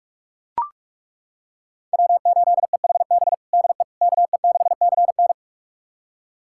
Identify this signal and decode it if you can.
Morse code, heard as a keyed tone, about 42 wpm, 700 Hz. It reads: W8EHB DE KE6CN
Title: USA Callsign DE Callsign - 5 Characters - 45wpm